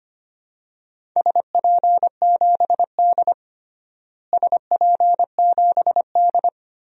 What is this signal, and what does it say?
Morse code, heard as a keyed tone, about 25 wpm, 700 Hz. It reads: SP7D SP7D
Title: DXCC-100 Call Signs - 4 Characters - Repeated 2x - 25wpm